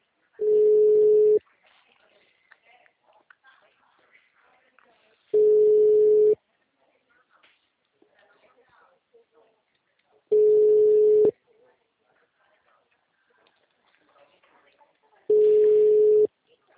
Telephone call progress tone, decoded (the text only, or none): ringback tone